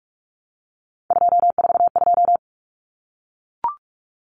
Morse code, heard as a keyed tone, about 44 wpm, 700 Hz. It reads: 242
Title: Numbers - 3-Digit 45wpm